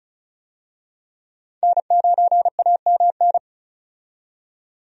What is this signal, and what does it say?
Morse code, heard as a keyed tone, about 36 wpm, 700 Hz. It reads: N9AMN
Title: USA Call Signs - 5 Characters - 35wpm